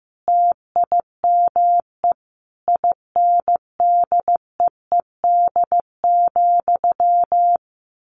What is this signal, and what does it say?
Morse code, heard as a keyed tone, about 15 wpm, 700 Hz. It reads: TIME INDEED,
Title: A Sound Of Thunder 15wpm